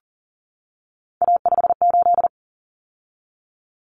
Morse code, heard as a keyed tone, about 40 wpm, 700 Hz. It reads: A58
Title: Common Cut Numbers - 3-Digit 40wpm